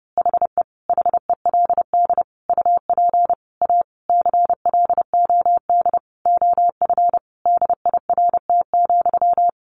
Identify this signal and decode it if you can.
Morse code, heard as a keyed tone, about 30 wpm, 700 Hz. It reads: HE HELD UP A CLOD OF DIRT,
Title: A Sound Of Thunder 30wpm